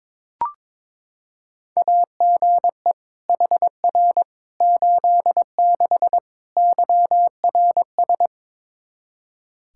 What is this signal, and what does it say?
Morse code, heard as a keyed tone, about 22 wpm, 700 Hz. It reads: AGE HR 86 YRS
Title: Practice QSO - Exchange 3 - More Info - 22wpm